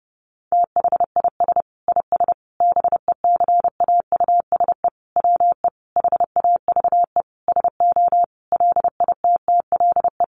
Morse code, heard as a keyed tone, about 30 wpm, 700 Hz. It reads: THIS IS BECAUSE WE HAVE SO LITTLE
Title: Sentences from Top 1200 Words 30wpm